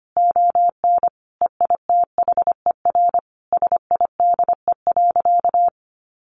Morse code, heard as a keyed tone, about 25 wpm, 700 Hz. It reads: ON EITHER SIDE.
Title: Sentences from Top 1200 Words 25wpm